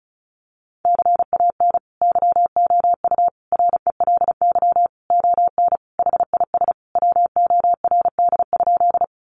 Morse code, heard as a keyed tone, about 34 wpm, 700 Hz. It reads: CAN YOU RELY ON HIS WORD?
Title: Sentences from Top 1200 Words 35wpm